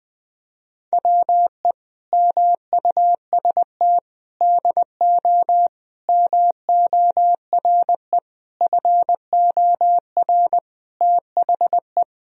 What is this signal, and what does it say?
Morse code, heard as a keyed tone, about 19 wpm, 700 Hz. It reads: WE MUST DO MORE FOR THE
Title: Sentences from Top 1000 Words - Review Part 3 - 20wpm